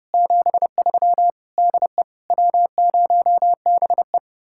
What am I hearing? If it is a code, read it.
Morse code, heard as a keyed tone, about 39 wpm, 700 Hz. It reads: 73 DE W0BE